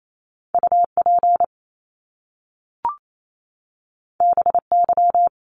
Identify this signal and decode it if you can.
Morse code, heard as a keyed tone, about 28 wpm, 700 Hz. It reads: UP BY
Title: Top Two Letter Combinations in Words 28wpm